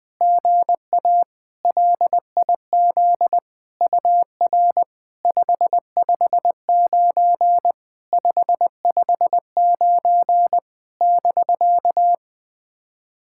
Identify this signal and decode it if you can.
Morse code, heard as a keyed tone, about 20 wpm, 700 Hz. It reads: GA LIZ UR 559 559 <BK>
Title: SOTA QSO Practice 20wpm